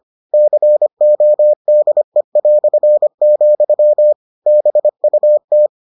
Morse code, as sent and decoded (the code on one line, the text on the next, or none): -.-. --- -.. .-..-. --..-- / -... ..- -
CODE", BUT